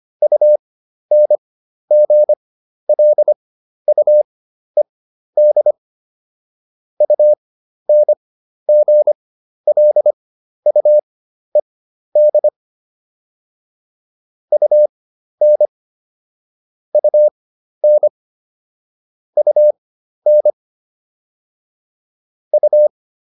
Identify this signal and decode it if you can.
Morse code, heard as a keyed tone, about 25 wpm, spaced as for 12 wpm, 600 Hz. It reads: UNGLUED UNGLUED UN UN UN U